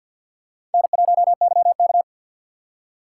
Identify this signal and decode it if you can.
Morse code, heard as a keyed tone, about 50 wpm, 700 Hz. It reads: N1YK
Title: USA Call Signs - 2x Character Spacing - 50wpm